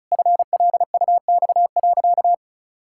Morse code, heard as a keyed tone, about 36 wpm, 700 Hz. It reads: FLUX.
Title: Pangram Challenge v2 - 35wpm